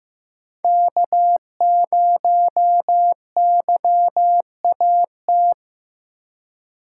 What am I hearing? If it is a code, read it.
Morse code, heard as a keyed tone, about 15 wpm, 700 Hz. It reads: K0YAT
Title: USA Call Signs - 5 Characters - 4x Character Spacing - 15wpm